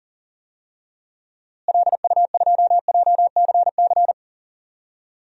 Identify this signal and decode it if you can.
Morse code, heard as a keyed tone, about 37 wpm, 700 Hz. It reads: LU2JCC
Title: DXCC-100 Call Signs - 6 Characters - Repeated 2x - 40wpm